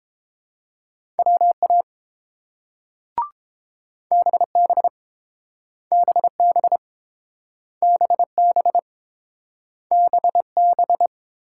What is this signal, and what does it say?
Morse code, heard as a keyed tone, about 30 wpm, 700 Hz. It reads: WA BB BB BB BB